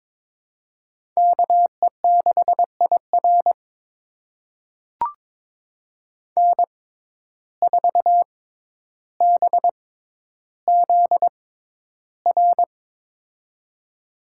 Morse code, heard as a keyed tone, about 22 wpm, spaced as for 11 wpm, 700 Hz. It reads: KE6IR N4BZR
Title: USA Call Signs - 5 Characters - 6x Character Spacing - 22wpm